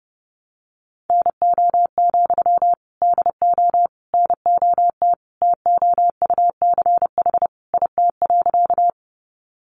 Morse code, heard as a keyed tone, about 30 wpm, 700 Hz. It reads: NO, DO NOT TOUCH IT.
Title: Sentences from Top 1200 Words 30wpm